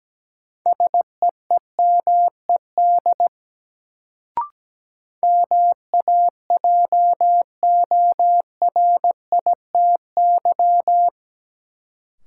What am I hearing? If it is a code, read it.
Morse code, heard as a keyed tone, about 17 wpm, 700 Hz. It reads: SEEMED MAJORITY
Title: Sets of 3 Words - New Word Drill - 17wpm